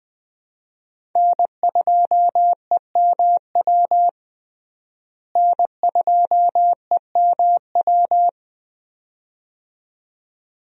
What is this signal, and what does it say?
Morse code, heard as a keyed tone, about 20 wpm, 700 Hz. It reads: N2EMW N2EMW
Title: USA Call Signs - 5 Characters - Repeated 2x - 20wpm